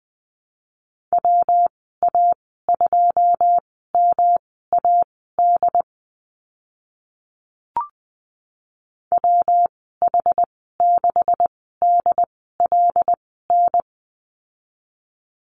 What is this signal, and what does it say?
Morse code, heard as a keyed tone, about 20 wpm, spaced as for 14 wpm, 700 Hz. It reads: WA2MAD WH6DLN
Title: 140 Lesson 23g - Letter f - Call Signs 20wpm